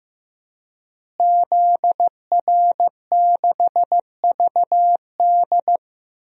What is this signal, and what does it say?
Morse code, heard as a keyed tone, about 15 wpm, 700 Hz. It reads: ZR6VD
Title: DXCC-100 Call Signs - 5 Characters - Repeated 2x - 15wpm